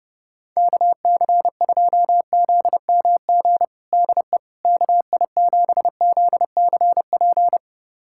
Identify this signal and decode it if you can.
Morse code, heard as a keyed tone, about 30 wpm, 700 Hz. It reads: KC2ZMG DE KI7ZCP
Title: USA Callsign DE Callsign - 6 Characters - 30wpm